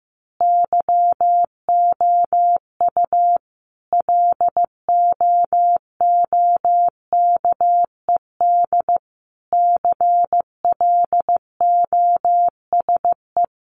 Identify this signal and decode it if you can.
Morse code, heard as a keyed tone, about 15 wpm, 700 Hz. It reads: YOU LOOKED CLOSE
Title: A Sound Of Thunder 15wpm